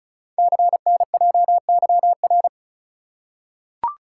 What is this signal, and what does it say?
Morse code, heard as a keyed tone, about 36 wpm, 700 Hz. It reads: CNJYR
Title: Code Groups - 5 Letters - 35wpm